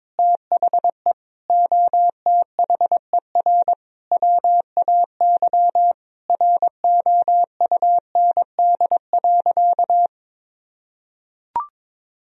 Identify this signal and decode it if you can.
Morse code, heard as a keyed tone, about 22 wpm, 700 Hz. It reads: THE OTHER WAY ROUND.
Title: Sentences from Top 1000 Words - Review Part 3 - 22wpm